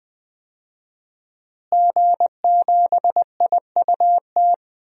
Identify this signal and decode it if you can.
Morse code, heard as a keyed tone, about 20 wpm, 700 Hz. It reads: G7IUT
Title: DXCC-100 Call Signs - 5 Characters - Repeated 2x - 20wpm